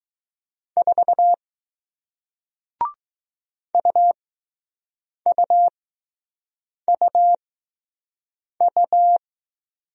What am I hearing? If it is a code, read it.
Morse code, heard as a keyed tone, about 20 wpm, spaced as for 6 wpm, 700 Hz. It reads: 4 UUUU